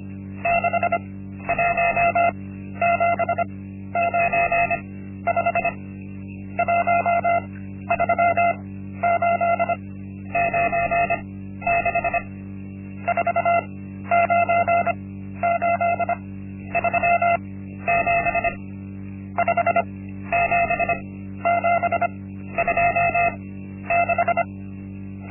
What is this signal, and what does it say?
Morse code, heard as a keyed tone, about 25 wpm, spaced as for 12 wpm, 690 Hz. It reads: 61795 13896 49837 57726
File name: Linksmakalnis-1